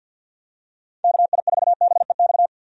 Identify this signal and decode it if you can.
Morse code, heard as a keyed tone, about 50 wpm, 700 Hz. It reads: KI4BEX
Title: USA Call Signs - 6 Characters - 50wpm